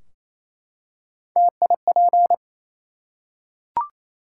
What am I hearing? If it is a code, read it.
Morse code, heard as a keyed tone, about 28 wpm, 700 Hz. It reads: TIP
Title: Three Letter Words - 28wpm